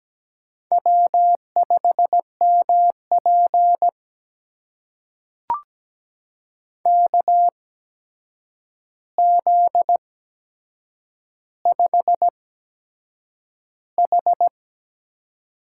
Morse code, heard as a keyed tone, about 17 wpm, spaced as for 7 wpm, 700 Hz. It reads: W5MP KZ5H
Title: USA Call Signs - 4 Characters - 8x Character Spacing - 17wpm